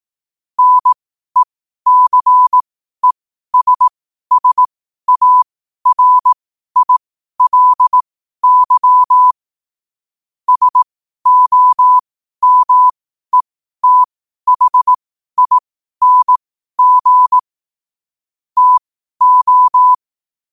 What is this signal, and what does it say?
Morse code, heard as a keyed tone, about 18 wpm, spaced as for 12 wpm, 1000 Hz. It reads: NECESSARILY SOMETHING TO
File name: qotdfiles / 12wpm